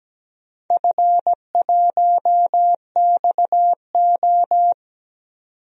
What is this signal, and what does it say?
Morse code, heard as a keyed tone, about 17 wpm, 700 Hz. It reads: F1XO
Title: DXCC-100 Call Signs - 4 Characters - 17wpm